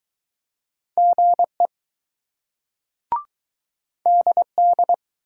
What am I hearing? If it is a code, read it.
Morse code, heard as a keyed tone, about 20 wpm, 700 Hz. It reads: GE DD